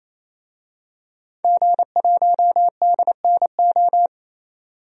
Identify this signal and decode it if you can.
Morse code, heard as a keyed tone, about 28 wpm, 700 Hz. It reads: G1DNO
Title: DXCC-100 Call Signs - 5 Characters - 28wpm